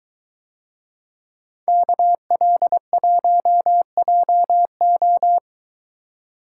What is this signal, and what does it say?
Morse code, heard as a keyed tone, about 23 wpm, 700 Hz. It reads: KL1JO